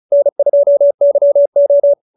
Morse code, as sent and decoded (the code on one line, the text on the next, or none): -. ..--- -.-- ---
N2YO